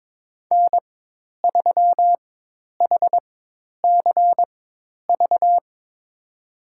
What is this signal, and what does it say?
Morse code, heard as a keyed tone, about 22 wpm, spaced as for 10 wpm, 700 Hz. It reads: N3HCV
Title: USA Call Signs - 5 Characters - 4x Character Spacing - 22wpm